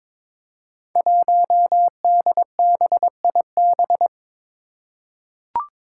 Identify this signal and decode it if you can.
Morse code, heard as a keyed tone, about 22 wpm, 700 Hz. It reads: 1DBIB
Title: Five Challenging Letter-Number Combinations 22wpm